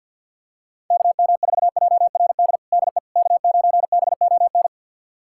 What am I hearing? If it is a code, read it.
Morse code, heard as a keyed tone, about 52 wpm, 700 Hz. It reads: KM4JRD DE K9BON
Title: USA Callsign DE Callsign 50wpm